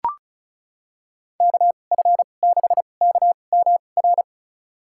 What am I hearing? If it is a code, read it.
Morse code, heard as a keyed tone, about 34 wpm, spaced as for 25 wpm, 700 Hz. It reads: KF6KMR
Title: USA Call Signs - 2x Character Spacing - 35wpm